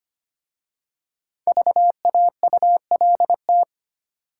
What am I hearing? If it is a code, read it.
Morse code, heard as a keyed tone, about 25 wpm, 690 Hz. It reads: VAULT
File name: practice / 25wpm / Easy Words with Hints - 25wpm